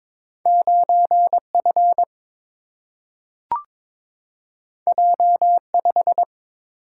Morse code, heard as a keyed tone, about 22 wpm, 700 Hz. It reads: 9 F J 5
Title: Single-Letter Single-Number 22wpm